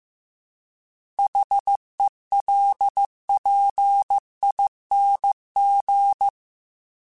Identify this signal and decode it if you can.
Morse code, heard as a keyed tone, about 15 wpm, 800 Hz. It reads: HELPING